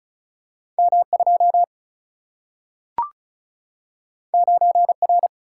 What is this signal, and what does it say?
Morse code, heard as a keyed tone, about 36 wpm, 700 Hz. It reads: M2 9R
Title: Single-Letter Single-Number 35wpm